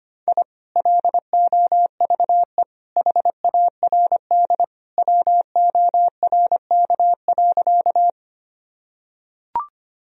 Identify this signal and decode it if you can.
Morse code, heard as a keyed tone, about 25 wpm, 700 Hz. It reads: I LOVE HARD WORK.
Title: Sentences from Top 1000 Words - Review Part 1 - 25wpm